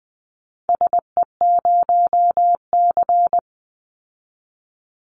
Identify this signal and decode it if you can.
Morse code, heard as a keyed tone, about 20 wpm, 700 Hz. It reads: SE0C
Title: Call Signs Repeated 3x - 20wpm